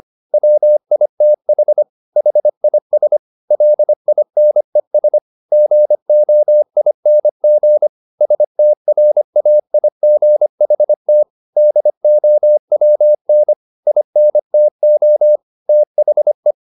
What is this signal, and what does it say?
Morse code, heard as a keyed tone, about 25 wpm, 600 Hz. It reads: WITH HIS LINES GOING STRAIGHT DOWN INTO THE
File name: the old man and the sea part 1 25wpm